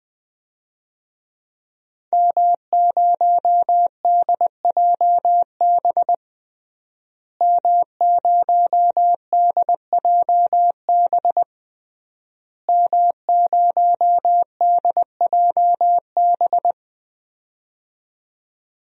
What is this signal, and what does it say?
Morse code, heard as a keyed tone, about 21 wpm, 700 Hz. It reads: M0DJB M0DJB M0DJB
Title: DXCC-100 Call Signs - 5 Characters - Repeated 3x - 20wpm